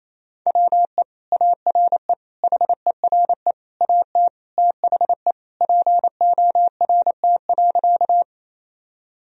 Morse code, heard as a keyed tone, about 28 wpm, 700 Hz. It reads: WE ARE HERE AT THE PORT.
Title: Sentences from Top 1000 Words - Review Part 1 - 28wpm